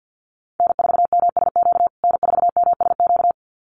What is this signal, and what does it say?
Morse code, heard as a keyed tone, about 50 wpm, 700 Hz. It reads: N4MSQ N4MSQ
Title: USA Call Signs Repeated 2x 50wpm